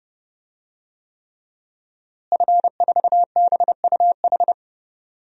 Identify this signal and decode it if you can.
Morse code, heard as a keyed tone, about 31 wpm, 700 Hz. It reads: F4BUH
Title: DXCC-100 Call Signs - 5 Characters - Repeated 3x - 30wpm